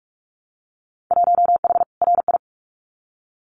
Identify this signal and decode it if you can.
Morse code, heard as a keyed tone, about 46 wpm, 700 Hz. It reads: 1H RI
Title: Winter Field Day Exchanges 45wpm